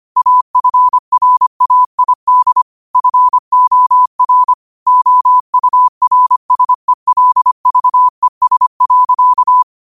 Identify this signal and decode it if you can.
Morse code, heard as a keyed tone, about 25 wpm, 1000 Hz. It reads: AFRAID FOR OURSELVES.